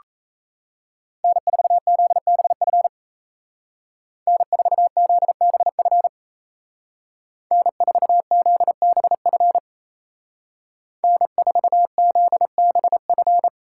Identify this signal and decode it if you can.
Morse code, heard as a keyed tone, about 36 wpm, 700 Hz. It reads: N4ZBF N4ZBF N4ZBF N4ZBF